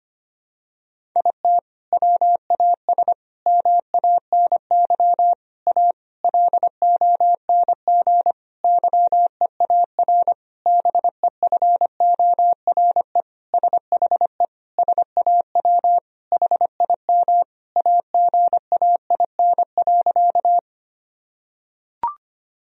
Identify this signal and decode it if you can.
Morse code, heard as a keyed tone, about 25 wpm, 700 Hz. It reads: IT WAS MANY A LONG YEAR BEFORE SHE SAW HIM AGAIN.
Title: Sentences from Top 1000 Words - Review Part 1 - 25wpm